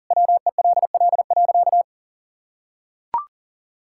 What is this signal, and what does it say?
Morse code, heard as a keyed tone, about 37 wpm, 700 Hz. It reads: WELL.
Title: Sentences from Top 1600 Words 40wpm